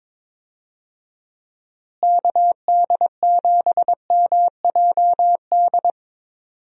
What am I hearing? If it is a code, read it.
Morse code, heard as a keyed tone, about 22 wpm, 700 Hz. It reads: KD7MJD
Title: USA Call Signs - 6 Characters - Repeated 3x - 22wpm